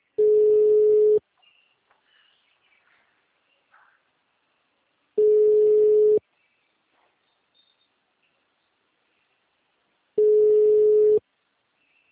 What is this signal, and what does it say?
Ringback tone (Europe, 425 Hz).
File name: out- -2012- - - .3245